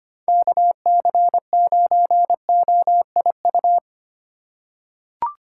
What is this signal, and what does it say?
Morse code, heard as a keyed tone, about 25 wpm, 700 Hz. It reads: KC9OIU
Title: USA Call Signs - 6 Characters - 3x Character Spacing - 25wpm